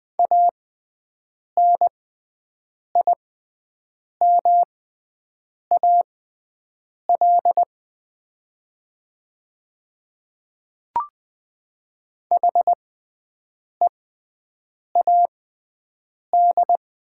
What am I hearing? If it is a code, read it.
Morse code, heard as a keyed tone, about 20 wpm, spaced as for 7 wpm, 700 Hz. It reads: ANIMAL HEAD
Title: 6x Character Spacing - Top 300 Words - 20wpm